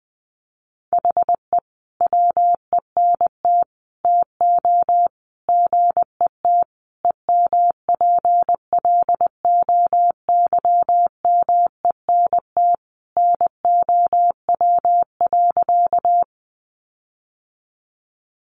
Morse code, heard as a keyed tone, about 20 wpm, 700 Hz. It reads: HE WENT TO GET EMPLOYMENT NOW.
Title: Sentences from Top 1500 Words 20wpm